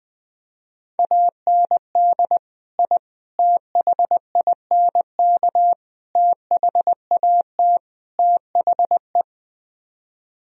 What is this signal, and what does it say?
Morse code, heard as a keyed tone, about 20 wpm, 700 Hz. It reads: AND I THINK THAT THE
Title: Speed-Racing - Sets of 5 Words 17wpm